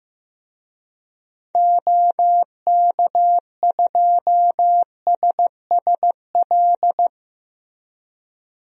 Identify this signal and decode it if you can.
Morse code, heard as a keyed tone, about 15 wpm, 700 Hz. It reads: OK2SSL
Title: DXCC-100 Call Signs - 6 Characters - Repeated 2x - 15wpm